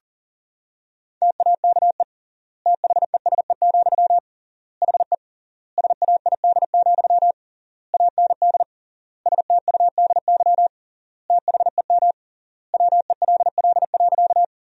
Morse code, heard as a keyed tone, about 40 wpm, 700 Hz. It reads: TAKE THESE, HE SAID, AND STUDY THEM WELL.
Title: 3x Word-Spacing - Sentences from Top 300 Words - 40wpm